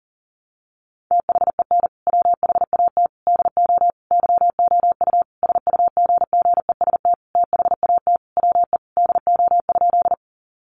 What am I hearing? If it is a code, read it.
Morse code, heard as a keyed tone, about 37 wpm, 700 Hz. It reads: THEN WHAT DO YOU SUGGEST THAT WE DO?
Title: Sentences from Top 1300 Words 40wpm